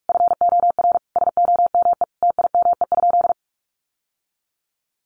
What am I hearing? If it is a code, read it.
Morse code, heard as a keyed tone, about 44 wpm, 700 Hz. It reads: FOR SOME TIME?
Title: Sentences from Top 600 Words 45wpm